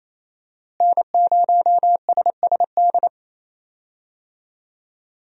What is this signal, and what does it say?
Morse code, heard as a keyed tone, about 27 wpm, 700 Hz. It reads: N0SSD
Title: USA Call Signs - 5 Characters - Repeated 3x - 28wpm